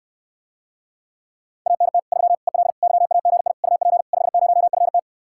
Morse code, heard as a keyed tone, about 69 wpm, 700 Hz. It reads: ANT 3 EL YAGI UP 50FT